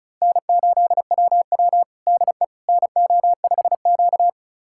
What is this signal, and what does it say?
Morse code, heard as a keyed tone, about 36 wpm, 700 Hz. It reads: N8WW DE NO5Q
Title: USA Callsign DE Callsign - 4 Characters - 35wpm